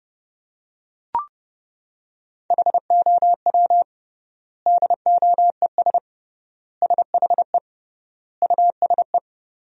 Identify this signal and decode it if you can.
Morse code, heard as a keyed tone, about 30 wpm, 700 Hz. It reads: HOW DOES SHE USE
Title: 3x Word-Spacing - Sentences from Top 200 Words - 30wpm